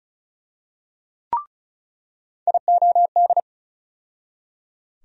Morse code, heard as a keyed tone, about 36 wpm, 700 Hz. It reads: IOD
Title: USA Call Sign Postfixes - 3 Letters 35wpm